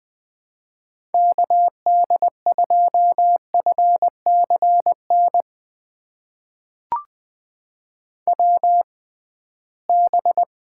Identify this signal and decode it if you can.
Morse code, heard as a keyed tone, about 20 wpm, spaced as for 15 wpm, 700 Hz. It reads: KD2FCN WB
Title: USA Call Signs - 6 Characters - 6x Character Spacing - 20wpm